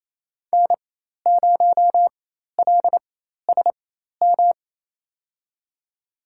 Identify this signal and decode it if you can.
Morse code, heard as a keyed tone, about 28 wpm, spaced as for 13 wpm, 700 Hz. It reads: N0LSM
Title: USA Call Signs - 5 Characters - 4x Character Spacing - 28wpm